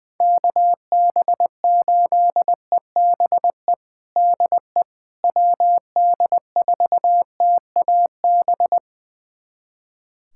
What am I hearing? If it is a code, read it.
Morse code, heard as a keyed tone, about 20 wpm, 700 Hz. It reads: KB8EBE DE WD4TAB
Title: USA Callsign DE Callsign - 6 Characters - 20wpm